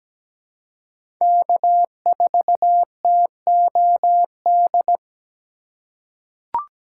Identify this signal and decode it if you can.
Morse code, heard as a keyed tone, about 17 wpm, 700 Hz. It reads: K4TOD
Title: USA Call Signs - 5 Characters - 6x Character Spacing - 17wpm